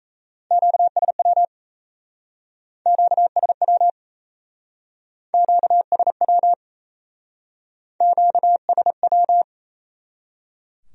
Morse code, heard as a keyed tone, about 37 wpm, 700 Hz. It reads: QSW QSW QSW QSW